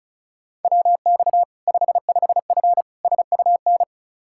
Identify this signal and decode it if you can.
Morse code, heard as a keyed tone, about 36 wpm, 700 Hz. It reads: WX 55F SUN
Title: Practice QSO - Exchange 2 - Info - 35wpm